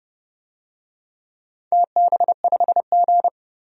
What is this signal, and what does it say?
Morse code, heard as a keyed tone, about 30 wpm, 700 Hz. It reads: TB5G
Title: DXCC-100 Call Signs - 4 Characters - 30wpm